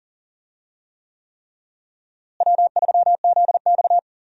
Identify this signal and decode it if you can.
Morse code, heard as a keyed tone, about 40 wpm, 700 Hz. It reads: W3ZX
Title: USA Call Signs - 4 Characters - Repeated 3x - 40wpm